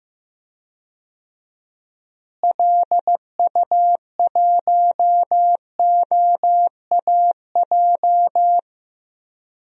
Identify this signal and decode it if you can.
Morse code, heard as a keyed tone, about 16 wpm, 700 Hz. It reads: LU1OAJ
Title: DXCC-100 Call Signs - 6 Characters - Repeated 2x - 15wpm